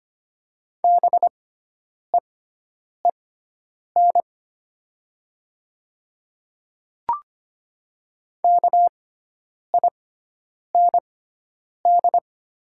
Morse code, heard as a keyed tone, about 25 wpm, spaced as for 9 wpm, 700 Hz. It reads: BEEN KIND
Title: 6x Character Spacing - Top 200 Words - 25wpm